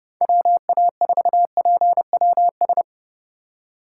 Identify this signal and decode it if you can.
Morse code, heard as a keyed tone, about 30 wpm, 700 Hz. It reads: WA4PWS